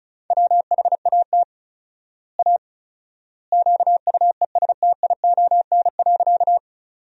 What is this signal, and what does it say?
Morse code, heard as a keyed tone, about 36 wpm, 700 Hz. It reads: WHAT A QUESTION.
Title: 4x Word-Spacing - Sentences from Top 400 Words - 35wpm